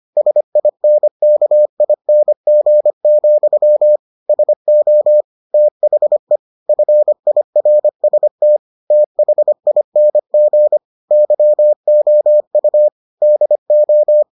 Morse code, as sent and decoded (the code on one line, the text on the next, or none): ... .. -. -.- .. -. --. --..-- / ... --- / - .... / ..-. .. .-. ... - / - .... .. -. --. / -.-- --- ..- / -.. ---
SINKING, SO THE FIRST THING YOU DO